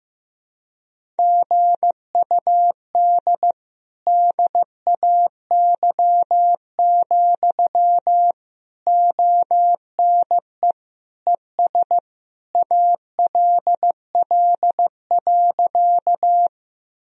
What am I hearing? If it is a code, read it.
Morse code, heard as a keyed tone, about 15 wpm, 700 Hz. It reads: GUD DAY, ONE ES ALL.
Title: Sentences from Top 200 Words and Abbreviations 15wpm